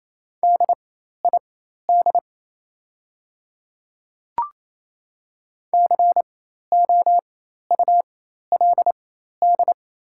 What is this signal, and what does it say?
Morse code, heard as a keyed tone, about 27 wpm, spaced as for 13 wpm, 690 Hz. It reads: DID COULD
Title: 4x Character Spacing - Top 100 Words - 28wpm